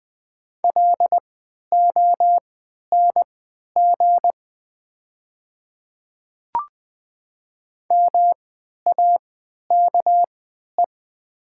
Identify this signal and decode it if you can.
Morse code, heard as a keyed tone, about 20 wpm, spaced as for 11 wpm, 700 Hz. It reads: LONG MAKE
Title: 3x Character Spacing - Top 100 Words - 20wpm